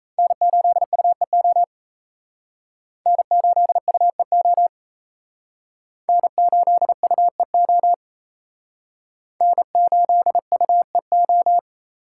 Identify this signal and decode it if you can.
Morse code, heard as a keyed tone, about 34 wpm, 700 Hz. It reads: N8UEO N8UEO N8UEO N8UEO
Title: Speed Racing - USA 1x3 Call Signs 28wpm